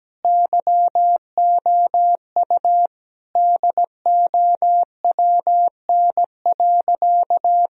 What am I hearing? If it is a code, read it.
Morse code, heard as a keyed tone, about 17 wpm, 700 Hz. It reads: YOU DOWN.
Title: Sentences from Top 1000 Words - Review Part 2 - 17wpm